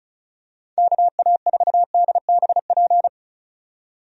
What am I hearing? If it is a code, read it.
Morse code, heard as a keyed tone, about 36 wpm, 700 Hz. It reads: KA4DBP
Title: USA Call Signs - 6 Characters - 35wpm